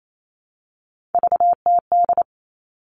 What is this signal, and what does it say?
Morse code, heard as a keyed tone, about 28 wpm, 700 Hz. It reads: VTD
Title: All Cut Numbers - 3-Digit 28wpm